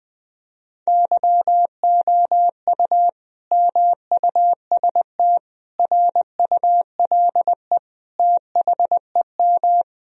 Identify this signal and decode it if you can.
Morse code, heard as a keyed tone, about 21 wpm, 700 Hz. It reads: YOU MUST RULE THEM
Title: Sentences from Top 1000 Words - Review Part 2 - 20wpm